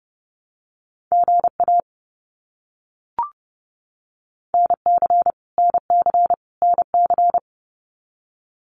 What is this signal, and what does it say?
Morse code, heard as a keyed tone, about 30 wpm, 700 Hz. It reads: GA NC NC NC
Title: US States Repeated 3x 30wpm